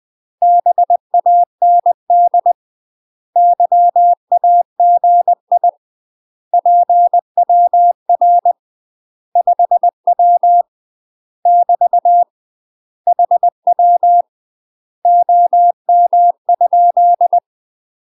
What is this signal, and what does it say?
Morse code, heard as a keyed tone, about 20 wpm, 700 Hz. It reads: BAND YAGI PWR 5W <BT> HW OM?
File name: MASTER QSOs TRAINING/20WPM